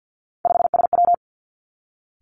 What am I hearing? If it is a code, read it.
Morse code, heard as a keyed tone, about 50 wpm, 700 Hz. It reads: 5SW
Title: Three Challenging Letter-Number Combinations 50wpm